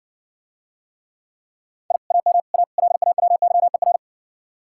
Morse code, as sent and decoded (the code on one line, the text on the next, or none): .. / .- -- / .- / .-.. .- .-- -.-- . .-.
I AM A LAWYER